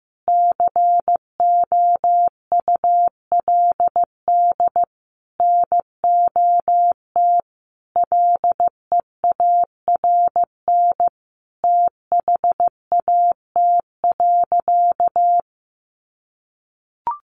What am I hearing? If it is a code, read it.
Morse code, heard as a keyed tone, about 15 wpm, 700 Hz. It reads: COULD NOT LEARN THAT.
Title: Sentences from Top 1200 Words 15wpm